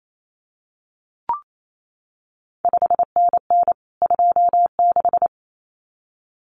Morse code, heard as a keyed tone, about 28 wpm, 700 Hz. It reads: 5NN 26
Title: CQ WW DX Contest Exchange Encore - 28wpm